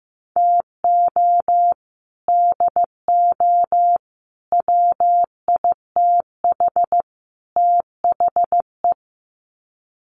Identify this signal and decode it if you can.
Morse code, heard as a keyed tone, about 15 wpm, 700 Hz. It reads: TO DO WITH THE
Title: Sets of 4 Words 15wpm